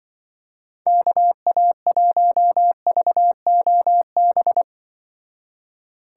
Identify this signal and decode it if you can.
Morse code, heard as a keyed tone, about 24 wpm, 700 Hz. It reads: KA1VOB
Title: Speed Racing - USA 2x3 Call Signs 20wpm